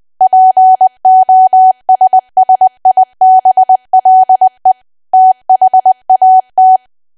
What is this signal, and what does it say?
Morse code, heard as a keyed tone, about 20 wpm, 750 Hz. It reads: POSSIBLE THAT
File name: cwa / W1AW-Level3 / 20WPM